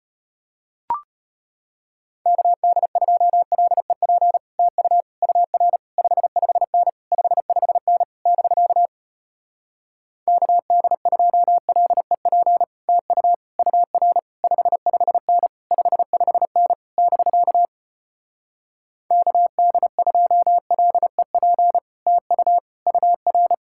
Morse code, heard as a keyed tone, about 33 wpm, 700 Hz. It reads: KD2LEP TU UR 55N 55N <BK> KD2LEP TU UR 55N 55N <BK> KD2LEP TU UR